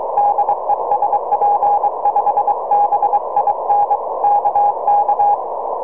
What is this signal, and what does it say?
Morse code, heard as a keyed tone, about 23 wpm, 820 Hz, with noise. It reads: DE SP5BIN KK
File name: tropo / sp5bin 23cm